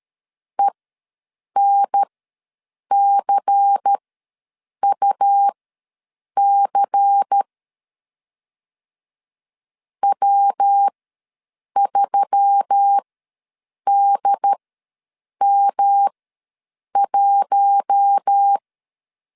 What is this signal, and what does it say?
Morse code, heard as a keyed tone, about 13 wpm, spaced as for 7 wpm, 800 Hz. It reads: ENCUC W3DM1
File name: LEZIONI / 5 LBVC / 5 LBVC ws 12wpm 3159